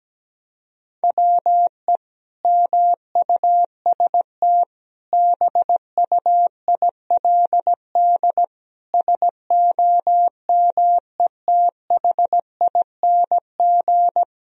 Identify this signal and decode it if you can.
Morse code, heard as a keyed tone, about 17 wpm, 700 Hz. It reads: WE MUST BUILD SOMETHING
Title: Sentences from Top 1000 Words - Review Part 1 - 17wpm